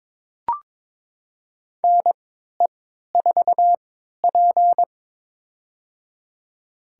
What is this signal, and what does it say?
Morse code, heard as a keyed tone, about 23 wpm, spaced as for 13 wpm, 700 Hz. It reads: NE4P
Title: USA Call Signs - 4 Characters - 3x Character Spacing - 22wpm